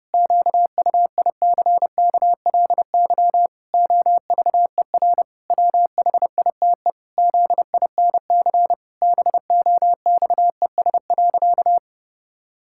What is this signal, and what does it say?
Morse code, heard as a keyed tone, about 30 wpm, 700 Hz. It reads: QUICKLY OVER WHITE ZINC BOXES.
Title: Pangram Challenge v2 - 30wpm